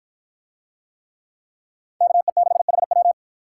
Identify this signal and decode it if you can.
Morse code, heard as a keyed tone, about 52 wpm, 700 Hz. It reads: KE6HW